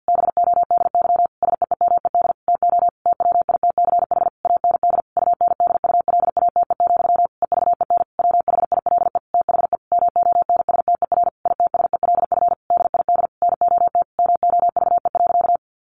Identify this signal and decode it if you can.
Morse code, heard as a keyed tone, about 50 wpm, 700 Hz. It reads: BODY SEEMED TO TWITCH AND UNDULATE, EVEN WHILE THE MONSTER ITSELF DID NOT MOVE.
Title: A Sound Of Thunder 50wpm